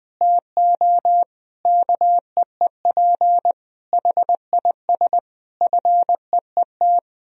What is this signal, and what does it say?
Morse code, heard as a keyed tone, about 20 wpm, 700 Hz. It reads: TO KEEP HIS FEET
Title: Sentences from Top 1000 Words - Review Part 3 - 20wpm